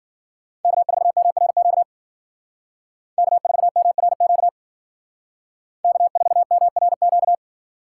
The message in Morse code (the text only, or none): K4MRQ K4MRQ K4MRQ